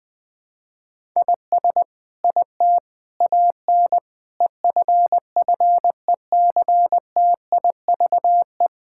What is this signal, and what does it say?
Morse code, heard as a keyed tone, about 20 wpm, 700 Hz. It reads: IS IT AN EFFECTIVE
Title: Sentences from Top 1000 Words - Review Part 6 - 20wpm